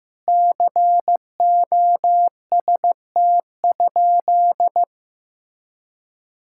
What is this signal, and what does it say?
Morse code, heard as a keyed tone, about 15 wpm, 700 Hz. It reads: COST?
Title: Sentences from Top 1000 Words - Review Part 5 - 15wpm